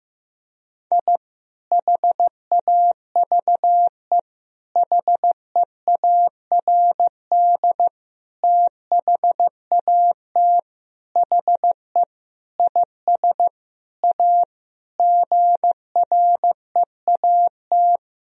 Morse code, heard as a keyed tone, about 15 wpm, 700 Hz. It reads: I HAVE HEARD THAT HE IS A GREAT
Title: Sentences from Top 1000 Words - Review Part 3 - 15wpm